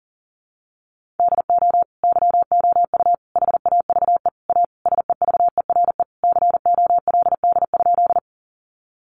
Morse code, heard as a keyed tone, about 40 wpm, 700 Hz. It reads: DO YOU HAVE A SEVERE COLD?
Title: Sentences from Top 1400 Words 40wpm